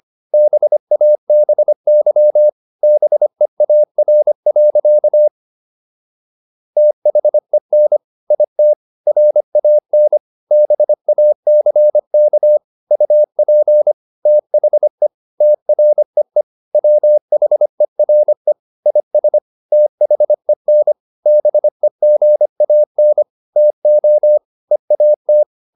Morse code, heard as a keyed tone, about 25 wpm, 600 Hz. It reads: BABY BEAR. THEN IT RAN BACK UP THE TREE WHERE IS THEN BEGAN TO EAT